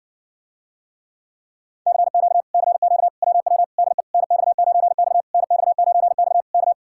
Morse code, heard as a keyed tone, about 60 wpm, 700 Hz. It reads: CQ CQ WW DE NX9X NX9X K